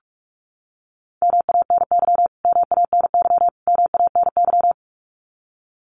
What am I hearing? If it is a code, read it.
Morse code, heard as a keyed tone, about 44 wpm, 700 Hz. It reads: MANY MANY MANY
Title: Top 200 Words Repeated 45wpm